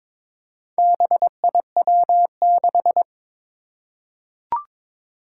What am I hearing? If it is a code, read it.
Morse code, heard as a keyed tone, about 22 wpm, 700 Hz. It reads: BIW6
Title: Four Challenging Letter-Number Combinations 22wpm